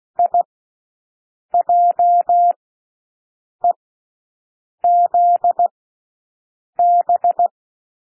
Morse code, heard as a keyed tone, about 16 wpm, spaced as for 6 wpm, 700 Hz. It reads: IJEZB
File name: cwmp3 / L1 SOUND LETTERS 16 wpm